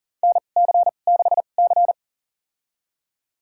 Morse code, heard as a keyed tone, about 38 wpm, spaced as for 28 wpm, 700 Hz. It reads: NC6C
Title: USA Call Signs - 4 Characters - 2x Character Spacing - 40wpm